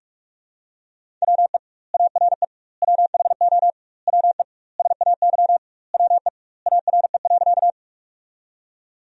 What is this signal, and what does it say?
Morse code, heard as a keyed tone, about 46 wpm, 700 Hz. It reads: WE ARE WHO WE SAY WE ARE.
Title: 2x Word-Spacing - Sentences from Top 200 Words - 45wpm